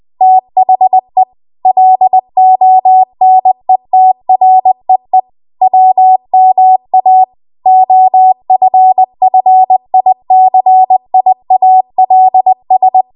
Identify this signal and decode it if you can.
Morse code, heard as a keyed tone, about 20 wpm, 750 Hz. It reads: THE LONETREE WMA OFFICIALS
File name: Archive / 20 WPM / 20WPM